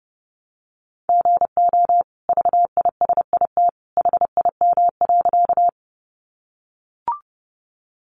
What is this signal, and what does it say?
Morse code, heard as a keyed tone, about 30 wpm, 700 Hz. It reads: GO VISIT HIM.
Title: Sentences from Top 900 Words 30wpm